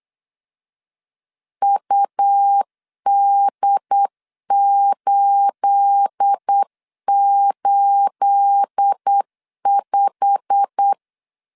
Morse code, heard as a keyed tone, about 8 wpm, 800 Hz. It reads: UD885
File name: CW / DET / 1 9 1500C S A 8wpm 5637